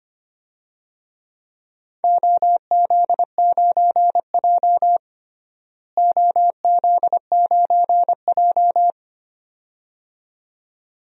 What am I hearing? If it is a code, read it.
Morse code, heard as a keyed tone, about 25 wpm, 700 Hz. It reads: OZ9J OZ9J
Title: DXCC-100 Call Signs - 4 Characters - Repeated 2x - 25wpm